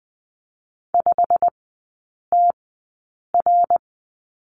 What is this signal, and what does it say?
Morse code, heard as a keyed tone, about 20 wpm, spaced as for 8 wpm, 700 Hz. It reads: 5TR
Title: 114 Lesson 19e - Letter w - Three Characters 20wpm